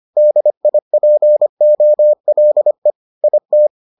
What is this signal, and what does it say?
Morse code, heard as a keyed tone, about 25 wpm, 600 Hz. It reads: DIPOLE IT